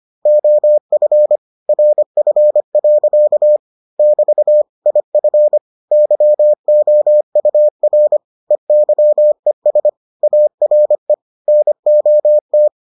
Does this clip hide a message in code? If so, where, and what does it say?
Yes: Morse code from 0.3 to 12.7 s, heard as a keyed tone, about 25 wpm, 600 Hz. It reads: OF RF. <BT> IF YOUR EYES ARE NOT